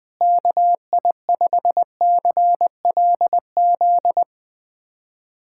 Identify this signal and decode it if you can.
Morse code, heard as a keyed tone, about 20 wpm, 700 Hz. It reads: KI5CLZ
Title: USA Callsign DE Callsign - 6 Characters - 20wpm